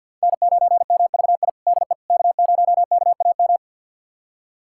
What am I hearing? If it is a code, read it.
Morse code, heard as a keyed tone, about 51 wpm, 700 Hz. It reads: N9MVI DE K0KAM
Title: USA Callsign DE Callsign - 5 Characters - 50wpm